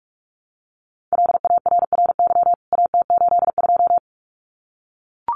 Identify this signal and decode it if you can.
Morse code, heard as a keyed tone, about 44 wpm, 700 Hz. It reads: LARRY AT82
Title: CWT Exchange 45wpm